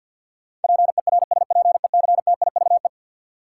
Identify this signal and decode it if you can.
Morse code, heard as a keyed tone, about 50 wpm, 700 Hz. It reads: PERSPECTIVE
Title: Sentences from Top 1200 Words - New Word Drill - 50wpm